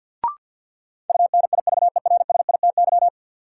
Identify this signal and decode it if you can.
Morse code, heard as a keyed tone, about 52 wpm, 700 Hz. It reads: UNIVERSITY
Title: Sentences from Top 1300 Words - New Word Drill Alt - 50wpm